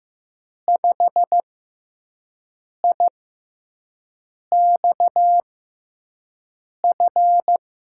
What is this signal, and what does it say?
Morse code, heard as a keyed tone, about 15 wpm, spaced as for 5 wpm, 700 Hz. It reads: 5IXF
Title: USA Call Signs - 6 Characters - 6x Character Spacing - 15wpm